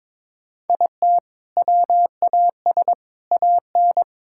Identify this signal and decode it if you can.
Morse code, heard as a keyed tone, about 22 wpm, 700 Hz. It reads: IT WAS AN
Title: Sentences from Top 1000 Words - Review Part 5 - 22wpm